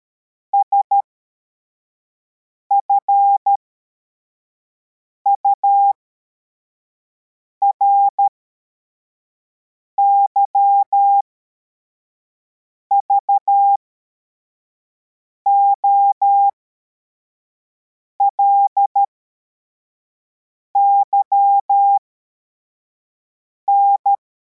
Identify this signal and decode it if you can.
Morse code, heard as a keyed tone, about 13 wpm, spaced as for 4 wpm, 800 Hz. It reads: SFURY VOLYN
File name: formation / cw / Text81